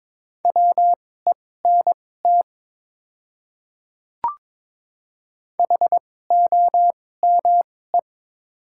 Morse code, heard as a keyed tone, about 22 wpm, spaced as for 16 wpm, 700 Hz. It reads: WENT HOME
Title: 2x Character Spacing - Top 300 Words - 22wpm